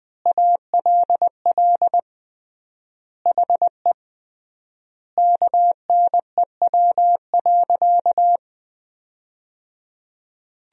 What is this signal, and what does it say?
Morse code, heard as a keyed tone, about 19 wpm, 700 Hz. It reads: ALL HE KNEW.
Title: 3x Word-Spacing - Sentences from Top 400 Words - 20wpm